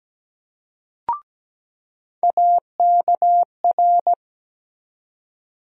Morse code, heard as a keyed tone, about 17 wpm, 700 Hz. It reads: AKR
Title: USA Call Sign Postfixes - 3 Letters 17wpm